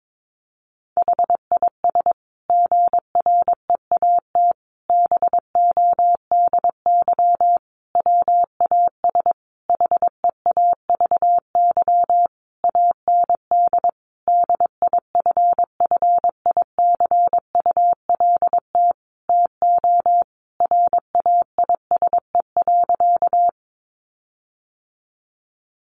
Morse code, heard as a keyed tone, about 22 wpm, 700 Hz. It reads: HIS GREAT BODY WAS HEAVY AND DIFFICULT TO RAISE.
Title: Sentences from Top 1400 Words 22wpm